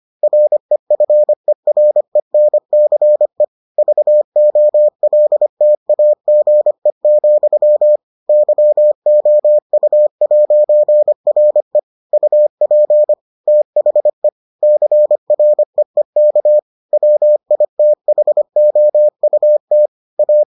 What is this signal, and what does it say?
Morse code, heard as a keyed tone, about 25 wpm, 600 Hz. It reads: REFERENCE VOLTAGE, YOU'RE UP THE CREEK WITHOUT A